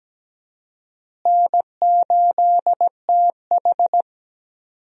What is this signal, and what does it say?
Morse code, heard as a keyed tone, about 17 wpm, 700 Hz. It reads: N8TH
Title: USA Call Signs - 4 Characters - Repeated 3x - 17wpm